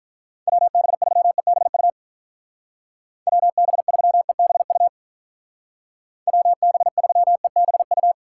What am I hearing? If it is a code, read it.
Morse code, heard as a keyed tone, about 46 wpm, 700 Hz. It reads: WB3EBU WB3EBU WB3EBU